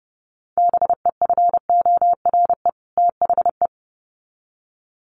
Morse code, heard as a keyed tone, about 30 wpm, 700 Hz. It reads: BEFORE THE
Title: Sets of 2 Words 30wpm